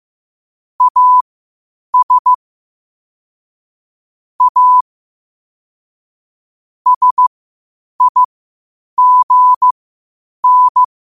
Morse code, heard as a keyed tone, about 15 wpm, spaced as for 8 wpm, 1000 Hz. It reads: AS A SIGN
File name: qotdfiles / 8wpm